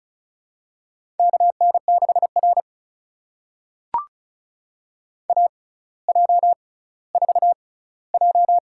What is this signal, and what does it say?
Morse code, heard as a keyed tone, about 36 wpm, spaced as for 18 wpm, 700 Hz. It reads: KN6R AJ4J
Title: USA Call Signs - 4 Characters - 6x Character Spacing - 35wpm